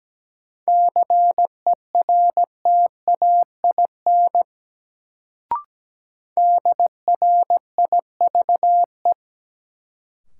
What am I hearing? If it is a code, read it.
Morse code, heard as a keyed tone, about 17 wpm, 700 Hz. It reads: CERTAIN DRIVE
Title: Sentences from Top 500 Words - New Word Drill - 17wpm